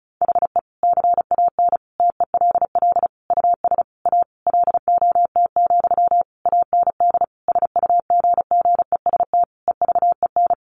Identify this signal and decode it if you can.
Morse code, heard as a keyed tone, about 34 wpm, 700 Hz. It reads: HE CAN TELL US A LOT, AND SUGGEST EVEN
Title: Sentences from Top 1300 Words 35wpm